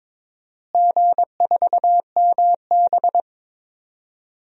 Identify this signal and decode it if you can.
Morse code, heard as a keyed tone, about 22 wpm, 700 Hz. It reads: G4MB
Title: DXCC-100 Call Signs - 4 Characters - Repeated 3x - 22wpm